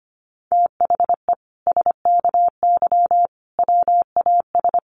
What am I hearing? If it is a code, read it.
Morse code, heard as a keyed tone, about 25 wpm, 700 Hz. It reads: THE SKY WAS
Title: Sentences from Top 1400 Words 25wpm